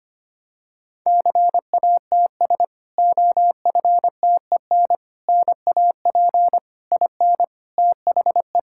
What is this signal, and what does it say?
Morse code, heard as a keyed tone, about 25 wpm, 700 Hz. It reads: CATS OFTEN NAP IN THE
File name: practice / 25wpm / Difficult Sentences with Hints - 25wpm